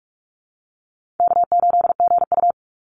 Morse code, heard as a keyed tone, about 44 wpm, 700 Hz. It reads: K8GU
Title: Call Signs - 45wpm